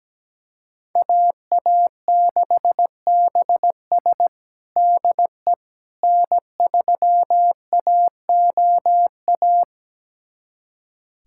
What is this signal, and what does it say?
Morse code, heard as a keyed tone, about 17 wpm, 700 Hz. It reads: AA6BS DE N3AOA
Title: USA Callsign DE Callsign - 5 Characters - 17wpm